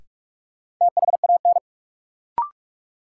Morse code, heard as a keyed tone, about 43 wpm, 700 Hz. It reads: THAN
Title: Sentences from Top 100 Words - New Word Drill - 45wpm